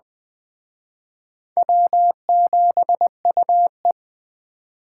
Morse code, heard as a keyed tone, about 20 wpm, 700 Hz. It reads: W7UE
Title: Speed Racing - USA 1x2 Call Signs 17wpm